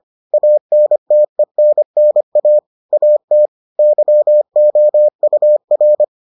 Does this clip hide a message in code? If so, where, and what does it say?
Yes: Morse code from 0.3 to 6.0 s, heard as a keyed tone, about 25 wpm, 600 Hz. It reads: ANTENNA AT YOUR